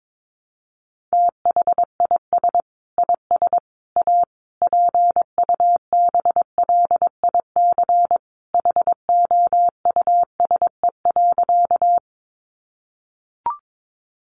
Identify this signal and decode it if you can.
Morse code, heard as a keyed tone, about 22 wpm, 700 Hz. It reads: THIS IS A PUBLIC HOUSE.
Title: Sentences from Top 600 Words 22wpm